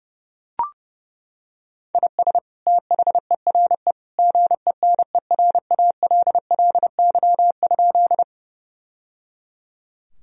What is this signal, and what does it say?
Morse code, heard as a keyed tone, about 30 wpm, 700 Hz. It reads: IS THERE GENERALLY?
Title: Sentences from Top 1000 Words - Review Part 4 - 30wpm